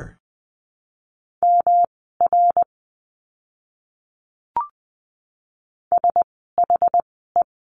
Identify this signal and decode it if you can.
Morse code, heard as a keyed tone, about 20 wpm, spaced as for 14 wpm, 700 Hz. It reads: MR SHE
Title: 244 Lesson 39f - Letter z - Words 20wpm